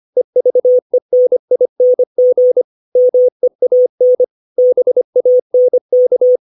Morse code, heard as a keyed tone, about 25 wpm, 500 Hz. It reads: EVENING MEAN BANK
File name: 632 words / 632 Words 25WPM